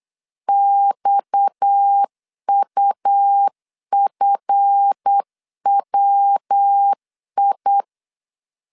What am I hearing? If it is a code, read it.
Morse code, heard as a keyed tone, about 8 wpm, 800 Hz. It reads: XUFWI
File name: LEZIONI / 7 GF9Z / 7 GF 8wpm 3108